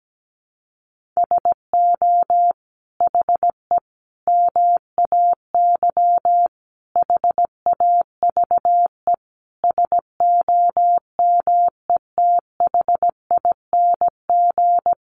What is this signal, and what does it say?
Morse code, heard as a keyed tone, about 17 wpm, 700 Hz. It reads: SO HE MAY HAVE SOMETHING
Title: Sentences from Top 1500 Words 17wpm